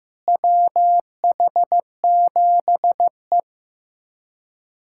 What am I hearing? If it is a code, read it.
Morse code, heard as a keyed tone, about 16 wpm, 700 Hz. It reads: WH7E
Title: USA Callsign DE Callsign - 4 Characters - 15wpm